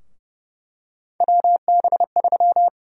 Morse code, heard as a keyed tone, about 30 wpm, 700 Hz. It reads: WB3
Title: USA Call Sign Prefixes - Repeated 2x - 30wpm